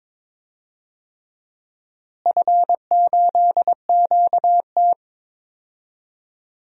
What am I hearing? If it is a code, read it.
Morse code, heard as a keyed tone, about 21 wpm, 700 Hz. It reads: F8QT
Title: DXCC-100 Call Signs - 4 Characters - Repeated 3x - 22wpm